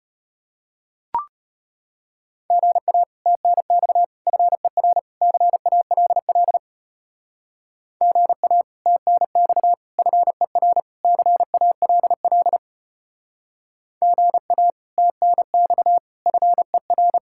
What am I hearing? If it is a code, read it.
Morse code, heard as a keyed tone, about 33 wpm, 700 Hz. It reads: GA TNX FER CALL GA TNX FER CALL GA TNX FER